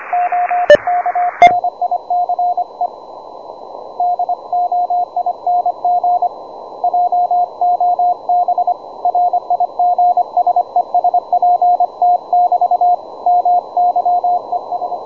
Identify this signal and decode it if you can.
Morse code, heard as a keyed tone, about 25 wpm, 710 Hz, with radio noise. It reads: OK NICE DOING JOB RIG SES PT <BT> MY ES
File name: juma-trx2 / juma-trx2-17m-cw-sample